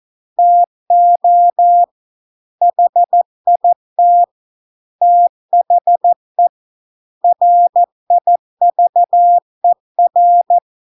Morse code, heard as a keyed tone, about 14 wpm, 700 Hz. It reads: TO HIT THE RIVER